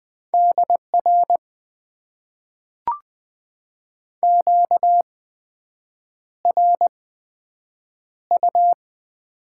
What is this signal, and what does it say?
Morse code, heard as a keyed tone, about 20 wpm, spaced as for 7 wpm, 700 Hz. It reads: DR QRU
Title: Top 90 QSO Elements - 8x Character Spacing - 20wpm